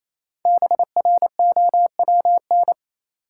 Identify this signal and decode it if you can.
Morse code, heard as a keyed tone, about 28 wpm, 700 Hz. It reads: BROWN
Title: Five Letter Words - 28wpm